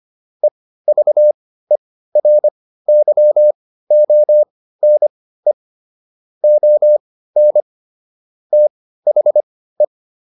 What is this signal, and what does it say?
Morse code, heard as a keyed tone, about 25 wpm, spaced as for 15 wpm, 600 Hz. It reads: EVERYONE ON THE
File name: SS101 15